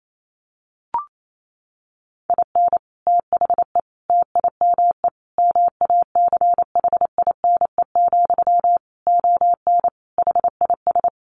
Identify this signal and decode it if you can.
Morse code, heard as a keyed tone, about 28 wpm, 700 Hz. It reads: IN THE TIME MACHINE, ON HIS
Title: A Sound Of Thunder 28wpm